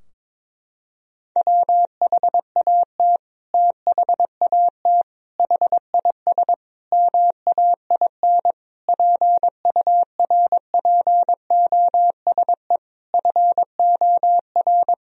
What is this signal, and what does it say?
Morse code, heard as a keyed tone, about 22 wpm, 700 Hz. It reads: WHAT THAT HIS MAIN PURPOSE FOR
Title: Sentences from Top 1000 Words - Review Part 2 - 22wpm